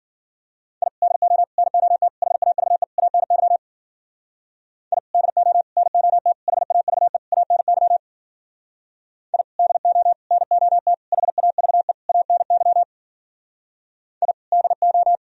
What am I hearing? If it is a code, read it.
Morse code, heard as a keyed tone, about 47 wpm, 700 Hz. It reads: I DO NOT HAVE ANY I DO NOT HAVE ANY I DO NOT HAVE ANY I DO
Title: Speed-Racing - Sets of 5 Words 40wpm